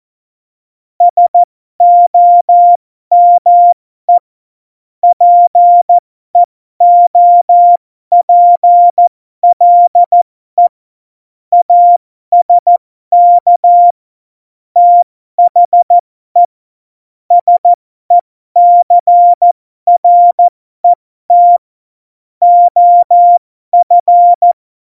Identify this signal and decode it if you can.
Morse code, heard as a keyed tone, about 14 wpm, spaced as for 12 wpm, 700 Hz. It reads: SOME PEOPLE ASK THE SECRET OF
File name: Morsekurs / Witze